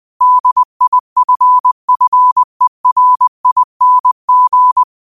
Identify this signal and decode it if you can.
Morse code, heard as a keyed tone, about 20 wpm, 1000 Hz. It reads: DIFFERING
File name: qotdfiles / 20wpm